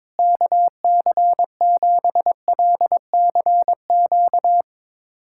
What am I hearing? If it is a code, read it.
Morse code, heard as a keyed tone, about 22 wpm, 700 Hz. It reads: KC7LCQ
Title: USA Callsign DE Callsign - 6 Characters - 22wpm